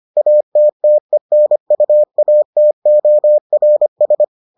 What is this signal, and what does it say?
Morse code, heard as a keyed tone, about 25 wpm, 600 Hz. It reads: ATTENUATORS